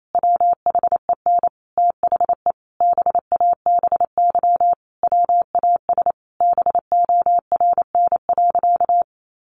Morse code, heard as a keyed tone, about 28 wpm, 700 Hz. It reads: WHEN THE BABY WAS BORN.
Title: Sentences from Top 1300 Words 28wpm